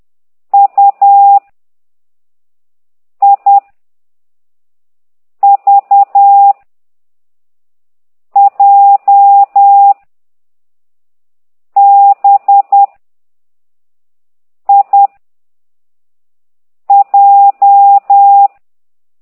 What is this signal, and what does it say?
Morse code, heard as a keyed tone, about 10 wpm, 800 Hz. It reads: U I V J B I J